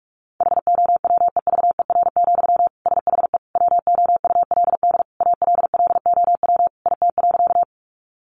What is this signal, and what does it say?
Morse code, heard as a keyed tone, about 44 wpm, 700 Hz. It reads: HOWEVER, SHE WOULD ALLOW IT.
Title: Sentences from Top 700 Words 45wpm